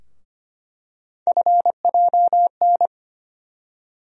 Morse code, heard as a keyed tone, about 25 wpm, 700 Hz. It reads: FJN
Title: USA Call Sign Postfixes - 3 Letters 25wpm